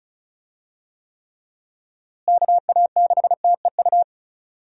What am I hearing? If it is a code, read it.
Morse code, heard as a keyed tone, about 36 wpm, 700 Hz. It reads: KA6TEU
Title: USA Call Signs - 6 Characters - Repeated 2x - 35wpm